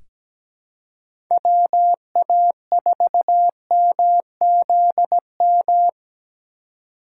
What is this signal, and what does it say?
Morse code, heard as a keyed tone, about 17 wpm, 700 Hz. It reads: WA4MZM
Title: USA Call Signs - 6 Characters - 4x Character Spacing - 17wpm